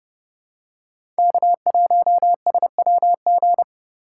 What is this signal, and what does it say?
Morse code, heard as a keyed tone, about 30 wpm, 700 Hz. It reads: K1SWG
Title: USA Call Signs - 5 Characters - 4x Character Spacing - 30wpm